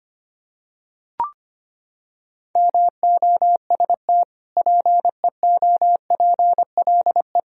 Morse code, heard as a keyed tone, about 25 wpm, 700 Hz. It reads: MOST PEOPLE
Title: Sentences from Top 1000 Words - Review Part 1 - 25wpm